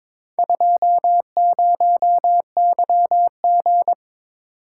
Morse code, heard as a keyed tone, about 22 wpm, 700 Hz. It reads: 20YG
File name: practice / 22wpm / Four Letter-Number Combinations - 22wpm